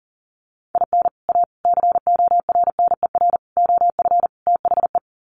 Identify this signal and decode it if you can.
Morse code, heard as a keyed tone, about 40 wpm, 700 Hz. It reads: IN A CORNER OF THE
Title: Sets of 5 Words Encore2 40wpm